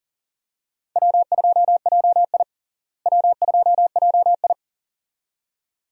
Morse code, heard as a keyed tone, about 40 wpm, 700 Hz. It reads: W2JI W2JI
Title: USA Call Signs - 4 Characters - Repeated 2x - 40wpm